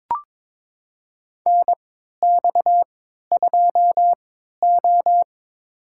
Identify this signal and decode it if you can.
Morse code, heard as a keyed tone, about 22 wpm, spaced as for 12 wpm, 700 Hz. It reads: NX2O
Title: USA Call Signs - 4 Characters - 3x Character Spacing - 22wpm